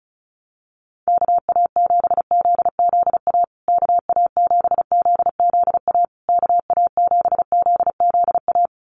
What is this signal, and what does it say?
Morse code, heard as a keyed tone, about 34 wpm, 700 Hz. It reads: KA7ZZA KA7ZZA KA7ZZA
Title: USA Call Signs Repeated 3x 35wpm